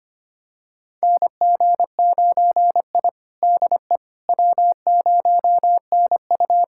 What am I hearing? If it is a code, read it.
Morse code, heard as a keyed tone, about 25 wpm, 700 Hz. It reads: NG9I DE W0NU
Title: USA Callsign DE Callsign - 4 Characters - 25wpm